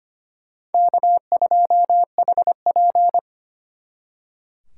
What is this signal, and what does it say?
Morse code, heard as a keyed tone, about 33 wpm, 700 Hz. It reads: K2HP